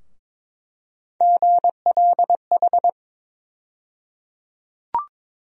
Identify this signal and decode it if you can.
Morse code, heard as a keyed tone, about 21 wpm, 700 Hz. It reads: GLH
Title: USA Call Sign Postfixes - 3 Letters 22wpm